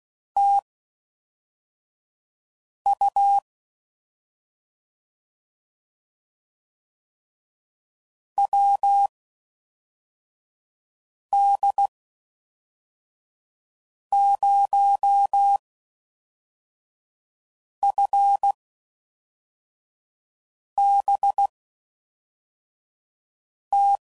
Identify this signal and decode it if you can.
Morse code, heard as a keyed tone, about 16 wpm, spaced as for 4 wpm, 800 Hz. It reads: TU WD0FBT